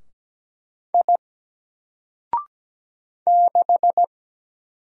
Morse code, heard as a keyed tone, about 17 wpm, 700 Hz. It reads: I 6
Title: Single Letter-Number Alt 17wpm